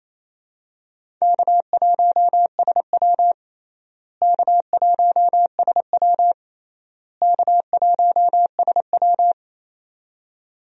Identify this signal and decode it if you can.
Morse code, heard as a keyed tone, about 28 wpm, 700 Hz. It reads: K1SW K1SW K1SW
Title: USA Call Signs - 4 Characters - Repeated 3x - 28wpm